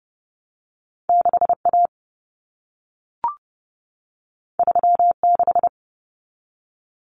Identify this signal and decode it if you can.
Morse code, heard as a keyed tone, about 30 wpm, 700 Hz. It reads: 6A 36
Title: Common Cut Numbers - 2-Digit 30wpm